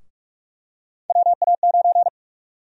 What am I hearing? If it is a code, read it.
Morse code, heard as a keyed tone, about 46 wpm, 700 Hz. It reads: WA9
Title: USA Call Sign Prefixes - 45wpm